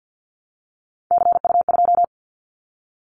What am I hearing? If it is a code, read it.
Morse code, heard as a keyed tone, about 50 wpm, 700 Hz. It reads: CU2
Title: Callsign Prefixes for Top 100 DXCC - 50wpm